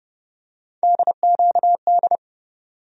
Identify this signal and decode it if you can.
Morse code, heard as a keyed tone, about 30 wpm, 700 Hz. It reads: DQD
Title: USA Call Sign Postfixes - 3 Letters 30wpm